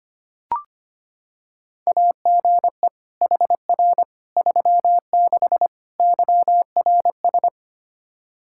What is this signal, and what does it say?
Morse code, heard as a keyed tone, about 25 wpm, 700 Hz. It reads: AGE HR 36 YRS
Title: Practice QSO - Exchange 3 - More Info - 25wpm